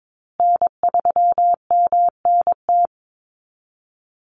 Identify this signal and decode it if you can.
Morse code, heard as a keyed tone, about 22 wpm, 700 Hz. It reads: N3MNT
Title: Call Signs Repeated 3x - 22wpm